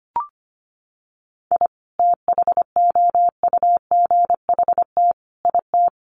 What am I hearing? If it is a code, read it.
Morse code, heard as a keyed tone, about 25 wpm, 700 Hz. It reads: I THOUGHT IT
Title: Sets of 4 Words 25wpm